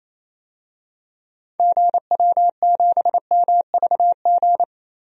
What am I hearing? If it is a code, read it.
Morse code, heard as a keyed tone, about 28 wpm, 700 Hz. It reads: GW7MVG
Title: DXCC-100 Call Signs - 6 Characters - 28wpm